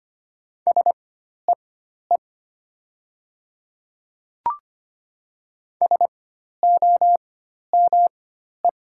Morse code, heard as a keyed tone, about 25 wpm, spaced as for 12 wpm, 700 Hz. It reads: SEE SOME
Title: 4x Character Spacing - Top 100 Words - 25wpm